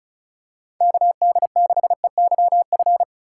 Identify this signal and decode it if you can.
Morse code, heard as a keyed tone, about 36 wpm, 700 Hz. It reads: KD6EYF
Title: USA Call Signs - 3x Character Spacing - 35wpm